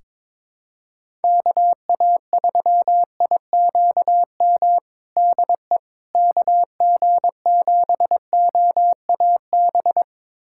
Morse code, heard as a keyed tone, about 22 wpm, 700 Hz. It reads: KA3IQM DE KG7OAB
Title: USA Callsign DE Callsign - 6 Characters - 22wpm